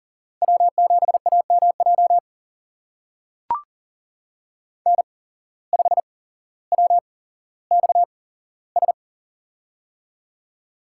Morse code, heard as a keyed tone, about 40 wpm, spaced as for 17 wpm, 700 Hz. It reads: W7AMJ N5WXS
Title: USA Call Signs - 5 Characters - 8x Character Spacing - 40wpm